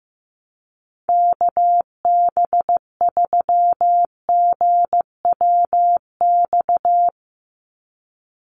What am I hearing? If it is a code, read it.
Morse code, heard as a keyed tone, about 15 wpm, 700 Hz. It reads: KB3GWX
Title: USA Call Signs 15wpm